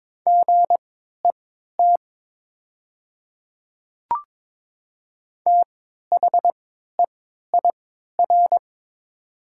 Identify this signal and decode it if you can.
Morse code, heard as a keyed tone, about 22 wpm, spaced as for 12 wpm, 700 Hz. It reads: GET THEIR
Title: 3x Character Spacing - Top 200 Words - 22wpm